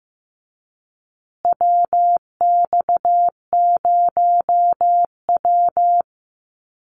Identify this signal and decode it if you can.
Morse code, heard as a keyed tone, about 15 wpm, 700 Hz. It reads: WX0W
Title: USA Call Signs Repeated 2x 15wpm